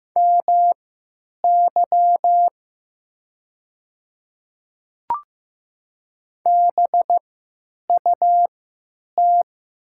Morse code, heard as a keyed tone, about 15 wpm, spaced as for 9 wpm, 700 Hz. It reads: MY BUT
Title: 3x Character Spacing - Top 100 Words - 15wpm